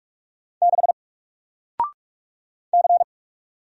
Morse code, heard as a keyed tone, about 44 wpm, 700 Hz. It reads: 6 C
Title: Single Character (Letter:Number:Punctuation:Prosign) Alt 45wpm